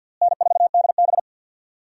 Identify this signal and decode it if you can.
Morse code, heard as a keyed tone, about 49 wpm, 700 Hz. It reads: N4DB
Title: USA Call Signs - 4 Characters - Repeated 2x - 50wpm